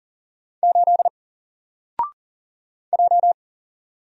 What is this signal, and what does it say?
Morse code, heard as a keyed tone, about 40 wpm, 700 Hz. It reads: 8 J
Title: Single Letter-Number 40wpm